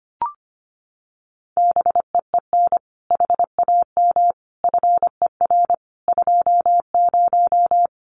Morse code, heard as a keyed tone, about 25 wpm, 700 Hz. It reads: BEEN HAM FER 20
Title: Common QSO Phrases 25wpm